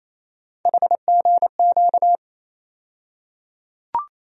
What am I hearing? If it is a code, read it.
Morse code, heard as a keyed tone, about 27 wpm, 700 Hz. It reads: HGQ
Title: USA Call Sign Postfixes - 3 Letters 28wpm